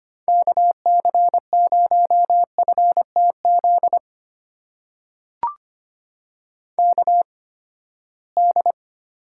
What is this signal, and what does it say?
Morse code, heard as a keyed tone, about 25 wpm, spaced as for 17 wpm, 700 Hz. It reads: KC0FTZ KD
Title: USA Call Signs - 6 Characters - 8x Character Spacing - 25wpm